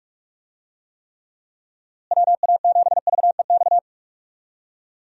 Morse code, heard as a keyed tone, about 46 wpm, 700 Hz. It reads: WA7VEX
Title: USA Call Signs - 6 Characters - Repeated 2x - 45wpm